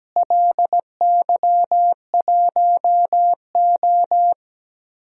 Morse code, heard as a keyed tone, about 17 wpm, 700 Hz. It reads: LY1O
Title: DXCC-100 Call Signs - 4 Characters - Repeated 3x - 17wpm